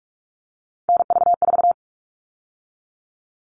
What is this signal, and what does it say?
Morse code, heard as a keyed tone, about 44 wpm, 700 Hz. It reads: NV4
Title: All Cut Numbers - 3-Digit 45wpm